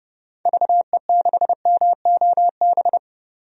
Morse code, heard as a keyed tone, about 30 wpm, 700 Hz. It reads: VE6MOB
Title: DXCC-100 Call Signs - 6 Characters - 30wpm